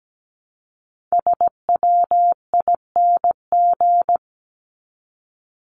Morse code, heard as a keyed tone, about 17 wpm, 700 Hz. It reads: SWING
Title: Top 1751-2000 Words 17wpm